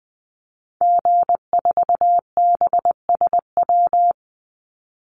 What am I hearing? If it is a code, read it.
Morse code, heard as a keyed tone, about 20 wpm, 700 Hz. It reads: G4BSW
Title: Call Signs Repeated 3x - 20wpm